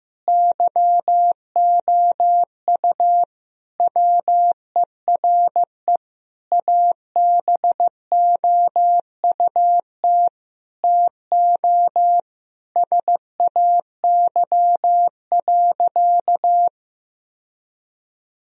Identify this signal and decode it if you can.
Morse code, heard as a keyed tone, about 15 wpm, 700 Hz. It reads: YOU WERE ABOUT TO SAY.
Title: Sentences from Top 1000 Words - Review Part 2 - 15wpm